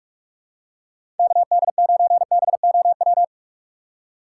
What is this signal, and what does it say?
Morse code, heard as a keyed tone, about 44 wpm, 700 Hz. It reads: KD9BOW
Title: USA Call Signs - 6 Characters - 4x Character Spacing - 45wpm